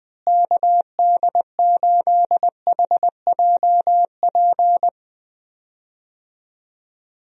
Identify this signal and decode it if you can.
Morse code, heard as a keyed tone, about 20 wpm, 700 Hz. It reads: KD8HJP
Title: USA Call Signs - 6 Characters - Repeated 2x - 20wpm